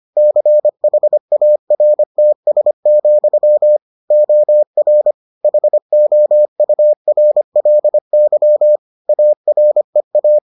Morse code, heard as a keyed tone, about 25 wpm, 600 Hz. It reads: CHARTS, OR HOURLY AREA